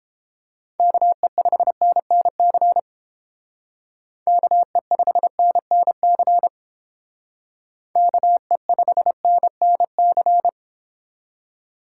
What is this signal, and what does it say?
Morse code, heard as a keyed tone, about 30 wpm, 700 Hz. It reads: KE5NNC KE5NNC KE5NNC